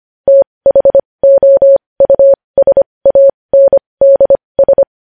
Morse code, heard as a keyed tone, about 25 wpm, spaced as for 20 wpm, 550 Hz. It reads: THOUSANDS
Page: news headlines in morse code at 20 wpm